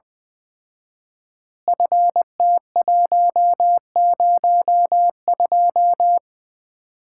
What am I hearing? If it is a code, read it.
Morse code, heard as a keyed tone, about 20 wpm, 700 Hz. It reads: FT102
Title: Common QSO Phrases - 4x Word-Spacing - 20wpm